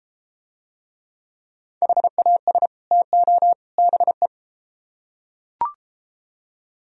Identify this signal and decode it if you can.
Morse code, heard as a keyed tone, about 33 wpm, 700 Hz. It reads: HAS TO BE